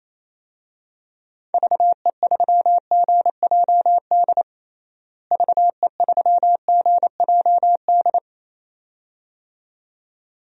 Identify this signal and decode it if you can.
Morse code, heard as a keyed tone, about 28 wpm, 700 Hz. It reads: VE3GJD VE3GJD
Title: DXCC-100 Call Signs - 6 Characters - Repeated 2x - 28wpm